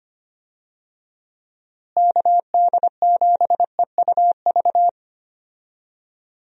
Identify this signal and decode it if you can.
Morse code, heard as a keyed tone, about 25 wpm, 700 Hz. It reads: KD7EUV
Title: USA Call Signs - 6 Characters - Repeated 3x - 25wpm